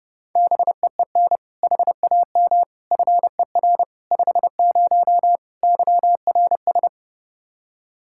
Morse code, heard as a keyed tone, about 32 wpm, 700 Hz. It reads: BEEN HAM FER 50 YRS
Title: Practice QSO - Exchange 3 - More Info - 30wpm